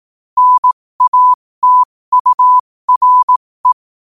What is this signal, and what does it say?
Morse code, heard as a keyed tone, about 18 wpm, spaced as for 15 wpm, 1000 Hz. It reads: NATURE
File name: qotdfiles / 15wpm